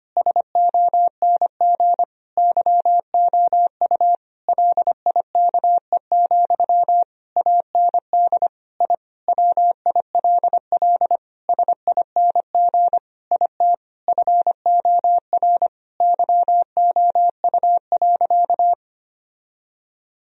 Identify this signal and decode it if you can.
Morse code, heard as a keyed tone, about 25 wpm, 700 Hz. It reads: SONG YOU LIKE, AND I WILL SING IT FOR YOU.
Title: Sentences from Top 1000 Words - Review Part 2 - 25wpm